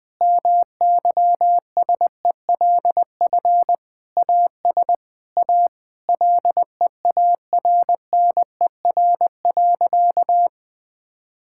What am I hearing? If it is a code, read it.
Morse code, heard as a keyed tone, about 19 wpm, 700 Hz. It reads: MYSELF AS A LEARNER.
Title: Sentences with 8-Word Phrases - 20wpm